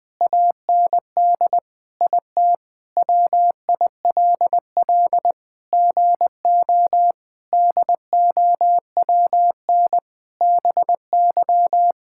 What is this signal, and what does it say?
Morse code, heard as a keyed tone, about 20 wpm, 700 Hz. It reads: AND IT WILL GO DOWN BY
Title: Sentences from Top 1000 Words - Review Part 5 - 20wpm